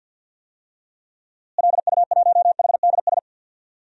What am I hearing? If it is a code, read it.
Morse code, heard as a keyed tone, about 52 wpm, 700 Hz. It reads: LU1HDS
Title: DXCC-100 Call Signs - 6 Characters - Repeated 3x - 50wpm